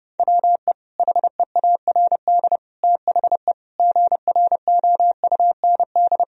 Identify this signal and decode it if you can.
Morse code, heard as a keyed tone, about 30 wpm, 700 Hz. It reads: WE HEARD THE GROUND
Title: Sentences from Top 1000 Words - Review Part 2 - 30wpm